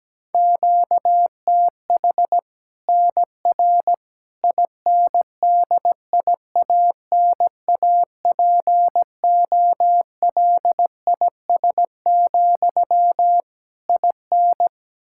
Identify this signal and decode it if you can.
Morse code, heard as a keyed tone, about 17 wpm, 700 Hz. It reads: QTH NR INDIANAPOLIS, IN
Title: Practice QSO - Exchange 1 - Beginning - 17wpm